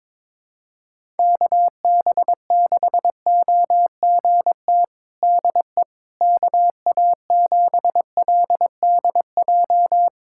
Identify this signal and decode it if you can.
Morse code, heard as a keyed tone, about 21 wpm, 700 Hz. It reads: KB6OGT DE KA7LDJ
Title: USA Callsign DE Callsign - 6 Characters - 22wpm